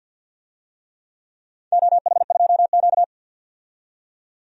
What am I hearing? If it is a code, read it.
Morse code, heard as a keyed tone, about 47 wpm, 700 Hz. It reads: OH2Q
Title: DXCC-100 Call Signs - 4 Characters - 50wpm